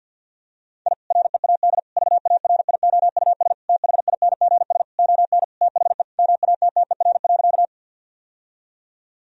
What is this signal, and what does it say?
Morse code, heard as a keyed tone, about 50 wpm, 700 Hz. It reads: I READ VARIOUS THINGS ON THE MATTER.
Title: Sentences from Top 1000 Words - Review Part 4 - 50wpm